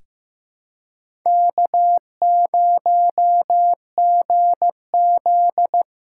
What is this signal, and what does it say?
Morse code, heard as a keyed tone, about 15 wpm, 700 Hz. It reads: K0GZ
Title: USA Callsign DE Callsign - 4 Characters - 15wpm